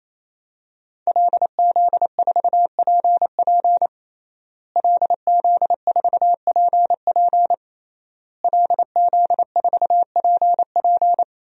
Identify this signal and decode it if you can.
Morse code, heard as a keyed tone, about 27 wpm, 700 Hz. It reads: LZ4PP LZ4PP LZ4PP
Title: DXCC-100 Call Signs - 5 Characters - Repeated 3x - 28wpm